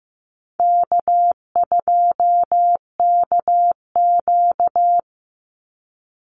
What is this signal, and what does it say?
Morse code, heard as a keyed tone, about 15 wpm, 700 Hz. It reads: K2KQ
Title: Call Signs Repeated 3x - 15wpm